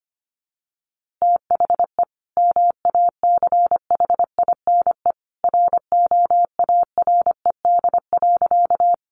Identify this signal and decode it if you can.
Morse code, heard as a keyed tone, about 25 wpm, 700 Hz. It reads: THE MACHINE ROARED.
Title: A Sound Of Thunder 25wpm